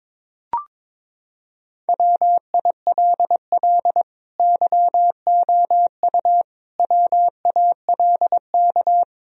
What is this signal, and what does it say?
Morse code, heard as a keyed tone, about 22 wpm, 700 Hz. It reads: WILL YOU WALK
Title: Sentences from Top 1000 Words - Review Part 1 - 22wpm